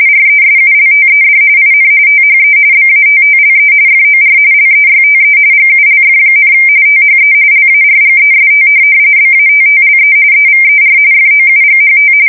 RTTY, about 45.45 baud. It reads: RESULTED IN THE BATTLE FOR LEYTE GULF, THE LARGEST BATTLE EVER FOUGHT
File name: fsk170 / USS-New-Jersey